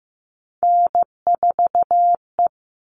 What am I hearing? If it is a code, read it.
Morse code, heard as a keyed tone, about 15 wpm, 700 Hz. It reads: N4E
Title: All Cut Numbers - 3-Digit 15wpm